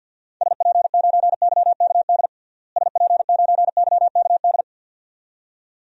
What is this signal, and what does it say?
Morse code, heard as a keyed tone, about 48 wpm, 700 Hz. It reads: SP9YKD SP9YKD
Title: DXCC-100 Call Signs - 6 Characters - Repeated 2x - 50wpm